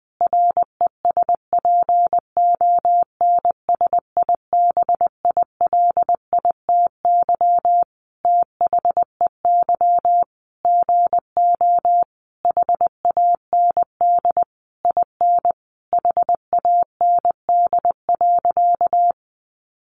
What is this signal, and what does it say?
Morse code, heard as a keyed tone, about 19 wpm, 700 Hz. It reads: RESPONSIBILITY THEY GO HAND IN HAND.
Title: Sentences from Top 1200 Words 20wpm